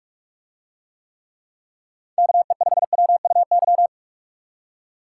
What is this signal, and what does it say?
Morse code, heard as a keyed tone, about 44 wpm, 700 Hz. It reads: KE5WUY